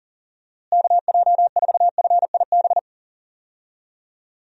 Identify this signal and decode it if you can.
Morse code, heard as a keyed tone, about 39 wpm, 700 Hz. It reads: KJ4FIB
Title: USA Call Signs - 6 Characters - Repeated 3x - 40wpm